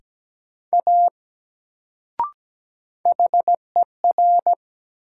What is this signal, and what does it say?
Morse code, heard as a keyed tone, about 17 wpm, 700 Hz. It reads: A HER
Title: Sentences from Top 100 Words - New Word Drill - 17wpm